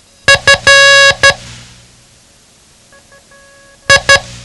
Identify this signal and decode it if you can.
Morse code, heard as a keyed tone, about 12 wpm, 1650 Hz. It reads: F I
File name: ham radio / CodeQuick / Code Quick 07